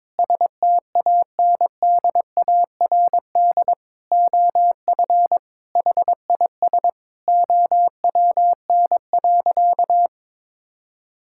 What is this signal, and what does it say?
Morse code, heard as a keyed tone, about 22 wpm, 700 Hz. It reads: STANDARD OF HIS OWN.
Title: Sentences from Top 1000 Words - Review Part 6 - 22wpm